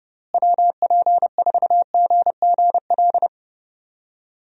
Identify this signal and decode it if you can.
Morse code, heard as a keyed tone, about 29 wpm, 700 Hz. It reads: WP4GGL
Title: USA Call Signs - 6 Characters - Repeated 3x - 30wpm